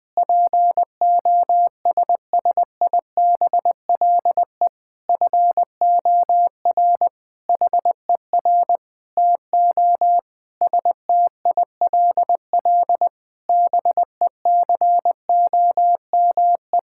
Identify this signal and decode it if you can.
Morse code, heard as a keyed tone, about 20 wpm, 700 Hz. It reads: POSSIBLE FOR HER TO STILL BECOME
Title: Sentences from Top 1000 Words - Review Part 6 - 20wpm